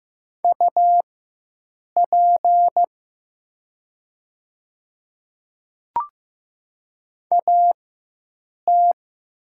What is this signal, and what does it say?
Morse code, heard as a keyed tone, about 15 wpm, spaced as for 7 wpm, 700 Hz. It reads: UP AT
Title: 4x Character Spacing - Top 100 Words - 15wpm